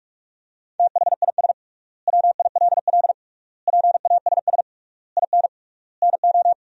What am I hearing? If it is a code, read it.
Morse code, heard as a keyed tone, about 46 wpm, 700 Hz. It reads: THIS WILL PASS IN NO
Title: 3x Word-Spacing - Sentences from Top 400 Words - 45wpm